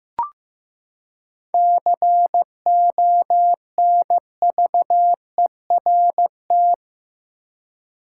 Morse code, heard as a keyed tone, about 15 wpm, 700 Hz. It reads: CONVERT
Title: Seven Letter Words - 15wpm